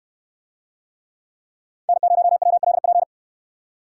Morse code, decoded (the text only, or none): N0WLP